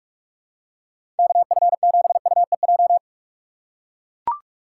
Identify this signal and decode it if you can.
Morse code, heard as a keyed tone, about 42 wpm, 700 Hz. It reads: KF7UEJ
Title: USA Call Signs - 6 Characters - 6x Character Spacing - 45wpm